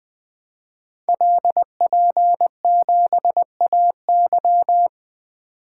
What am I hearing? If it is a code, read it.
Morse code, heard as a keyed tone, about 20 wpm, 700 Hz. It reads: LP7AY
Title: Code Groups - 5 Letter:Numbers - 20wpm